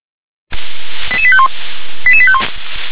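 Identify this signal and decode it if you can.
Five-tone selective call (ZVEI), heard as quick runs of five tones, tones of about 70 ms. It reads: 88851 88851
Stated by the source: ZVEI five-tone selective call 88148 received on an SDR.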